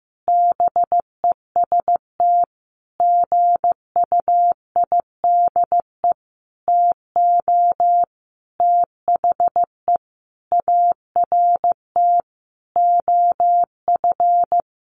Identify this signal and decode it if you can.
Morse code, heard as a keyed tone, about 15 wpm, 700 Hz. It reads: BEST GUIDE TO THE ART OF
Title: Sentences from Top 1200 Words 15wpm